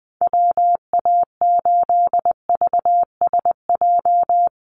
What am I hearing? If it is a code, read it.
Morse code, heard as a keyed tone, about 20 wpm, 700 Hz. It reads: WA8VSJ
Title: Call Signs - 20wpm